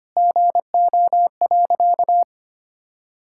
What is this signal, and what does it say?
Morse code, heard as a keyed tone, about 25 wpm, 700 Hz. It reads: GO.
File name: practice / 25wpm / Sentences with 4-Word Phrases - 25wpm